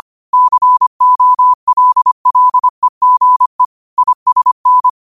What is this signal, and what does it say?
Morse code, heard as a keyed tone, about 25 wpm, 1000 Hz. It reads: COLLEGE ISN